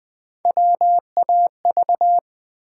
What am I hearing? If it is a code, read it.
Morse code, heard as a keyed tone, about 20 wpm, 700 Hz. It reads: WAV
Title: Build A Word - 1 Letter At A Time - 4-Letter Words - 20wpm